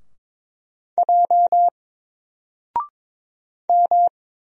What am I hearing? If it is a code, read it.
Morse code, heard as a keyed tone, about 22 wpm, 700 Hz. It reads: J M
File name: Single-Letter-Number-22wpm